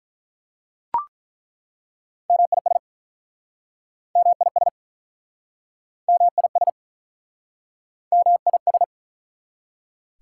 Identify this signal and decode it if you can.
Morse code, heard as a keyed tone, about 49 wpm, 700 Hz. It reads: MIS MIS MIS MIS